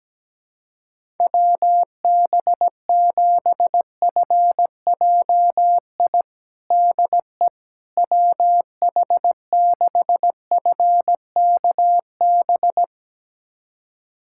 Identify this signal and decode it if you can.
Morse code, heard as a keyed tone, about 17 wpm, 700 Hz. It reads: WB7FJI DE WH6FKB
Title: USA Callsign DE Callsign - 6 Characters - 17wpm